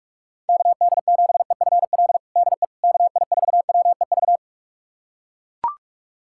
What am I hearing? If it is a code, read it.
Morse code, heard as a keyed tone, about 42 wpm, 700 Hz. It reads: KD7EFL DE KI4WEV
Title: USA Callsign DE Callsign - 6 Characters - 45wpm